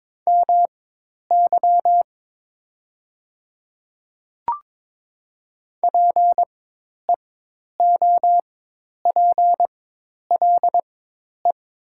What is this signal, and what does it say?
Morse code, heard as a keyed tone, about 23 wpm, spaced as for 10 wpm, 700 Hz. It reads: MY PEOPLE
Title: 4x Character Spacing - Top 100 Words - 22wpm